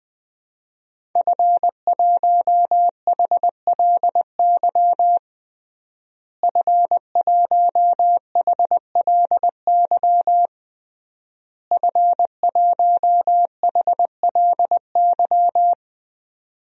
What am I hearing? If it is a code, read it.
Morse code, heard as a keyed tone, about 21 wpm, 700 Hz. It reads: F1HLY F1HLY F1HLY
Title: DXCC-100 Call Signs - 5 Characters - Repeated 3x - 20wpm